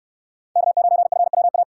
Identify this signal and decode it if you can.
Morse code, heard as a keyed tone, about 68 wpm, 700 Hz. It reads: K9FWA